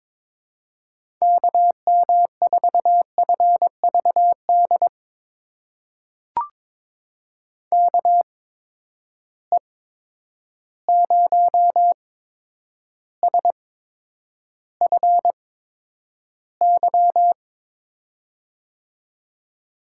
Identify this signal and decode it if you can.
Morse code, heard as a keyed tone, about 22 wpm, spaced as for 9 wpm, 700 Hz. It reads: KM4FVD KE0SFY
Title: USA Call Signs - 6 Characters - 8x Character Spacing - 22wpm